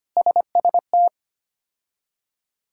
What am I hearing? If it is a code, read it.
Morse code, heard as a keyed tone, about 25 wpm, 700 Hz. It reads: SST
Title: K1USN SST QSO Practice - 25wpm